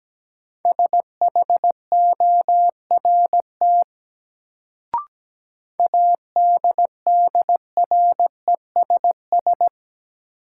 Morse code, heard as a keyed tone, about 17 wpm, 700 Hz. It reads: SHORT ADDRESS
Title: Sets of 5 Words Encore - New Word Drill - 17wpm